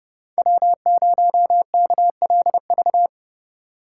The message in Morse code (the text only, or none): W0KLV